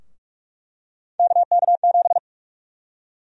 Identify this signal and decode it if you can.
Morse code, heard as a keyed tone, about 42 wpm, 700 Hz. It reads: KK7
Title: USA Call Sign Prefixes - 45wpm